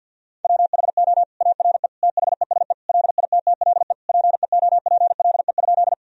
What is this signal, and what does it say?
Morse code, heard as a keyed tone, about 50 wpm, 700 Hz. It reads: WHO ARE THESE LITTLE PEOPLE?
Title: Sentences from Top 1000 Words - Review Part 1 - 50wpm